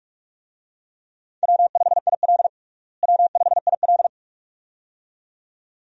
Morse code, heard as a keyed tone, about 42 wpm, 700 Hz. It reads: W5IL W5IL
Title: USA Call Signs - 4 Characters - Repeated 2x - 45wpm